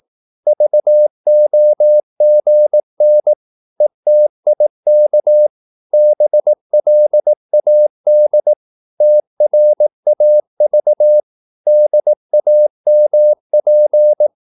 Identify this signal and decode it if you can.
Morse code, heard as a keyed tone, about 18 wpm, 600 Hz. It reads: VOGN ETIK BLAD TRAV DAMP
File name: Korte ord DK 18wpm